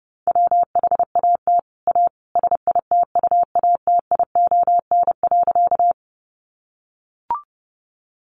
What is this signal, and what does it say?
Morse code, heard as a keyed tone, about 30 wpm, 700 Hz. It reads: WHAT A SITUATION.
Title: Sentences from Top 1200 Words 30wpm